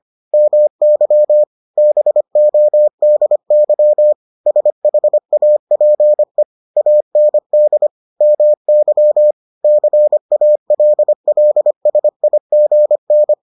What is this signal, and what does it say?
Morse code, heard as a keyed tone, about 25 wpm, 600 Hz. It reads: MY BODY SHAPE AND MY CALLSIGN